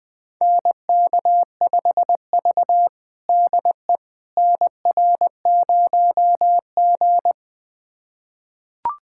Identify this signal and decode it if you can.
Morse code, heard as a keyed tone, about 20 wpm, 700 Hz. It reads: NK5V DE NR0G
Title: USA Callsign DE Callsign - 4 Characters - 20wpm